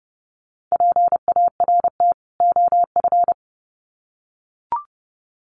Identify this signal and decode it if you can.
Morse code, heard as a keyed tone, about 30 wpm, 700 Hz. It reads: PART OF
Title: Sets of 2 Words 30wpm